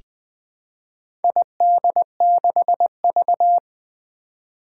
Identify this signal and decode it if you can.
Morse code, heard as a keyed tone, about 20 wpm, 700 Hz. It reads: ID6V
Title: Four Challenging Letter-Number Combinations 20wpm